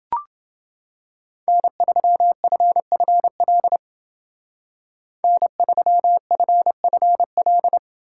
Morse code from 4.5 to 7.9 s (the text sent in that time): N3FFL